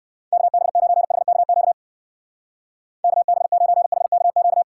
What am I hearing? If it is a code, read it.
Morse code, heard as a keyed tone, about 57 wpm, 700 Hz. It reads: KB9HKQ KB9HKQ